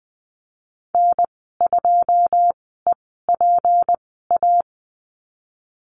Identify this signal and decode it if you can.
Morse code, heard as a keyed tone, about 20 wpm, spaced as for 14 wpm, 700 Hz. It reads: N2EPA
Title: 186 Lesson 30g - Letter b - Call Signs 20wpm